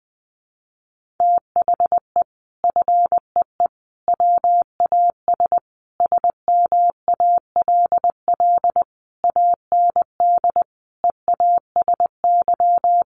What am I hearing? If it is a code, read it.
Morse code, heard as a keyed tone, about 20 wpm, 700 Hz. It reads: THE FEE WAS SMALL AND EASY
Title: Sentences from Top 1500 Words 20wpm